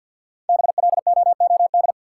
Morse code, heard as a keyed tone, about 53 wpm, 700 Hz. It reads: BLOOD
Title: Sentences from Top 1000 Words - Review Part 2 - New Word Drill - 50wpm